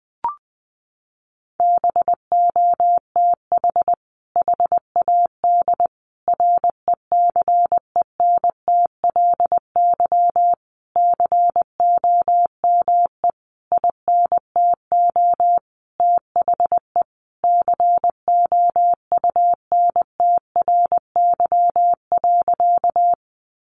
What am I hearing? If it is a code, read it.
Morse code, heard as a keyed tone, about 20 wpm, 700 Hz. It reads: BOTH HAD RECENTLY COME INTO THE COUNTRY.
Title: Sentences from Top 1200 Words 20wpm